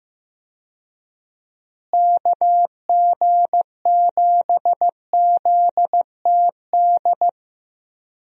Morse code, heard as a keyed tone, about 15 wpm, 700 Hz. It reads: KG7ZTD
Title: USA Call Signs - 6 Characters - Repeated 3x - 15wpm